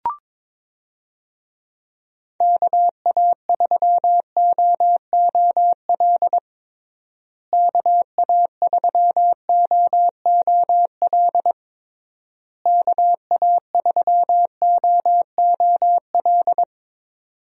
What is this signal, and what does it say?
Morse code, heard as a keyed tone, about 22 wpm, 700 Hz. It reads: KA3OOL KA3OOL KA3OOL
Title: USA Call Signs - 6 Characters - Repeated 3x - 22wpm